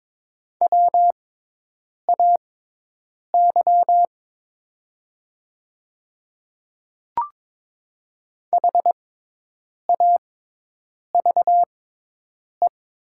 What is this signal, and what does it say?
Morse code, heard as a keyed tone, about 22 wpm, spaced as for 8 wpm, 700 Hz. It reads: WAY HAVE
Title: 6x Character Spacing - Top 200 Words - 22wpm